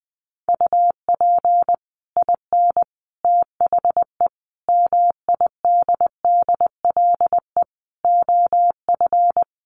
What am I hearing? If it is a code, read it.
Morse code, heard as a keyed tone, about 20 wpm, 700 Hz. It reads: UP IN THE MIDDLE OF
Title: Sets of 5 Words Encore 20wpm